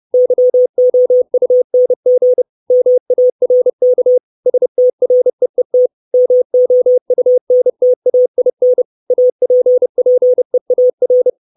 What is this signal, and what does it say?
Morse code, heard as a keyed tone, about 30 wpm, 500 Hz. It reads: YOUNG MARK STREET MOUNTAIN APPEAR
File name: 632 Words 30WPM